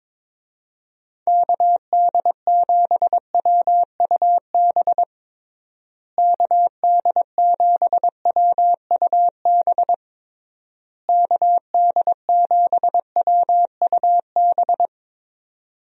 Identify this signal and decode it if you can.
Morse code, heard as a keyed tone, about 23 wpm, 700 Hz. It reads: KD7WUB KD7WUB KD7WUB
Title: USA Call Signs - 6 Characters - Repeated 3x - 22wpm